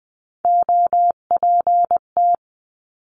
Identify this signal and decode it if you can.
Morse code, heard as a keyed tone, about 20 wpm, 700 Hz. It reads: OPT
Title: 259 Lesson 42d - Words 18-20wpm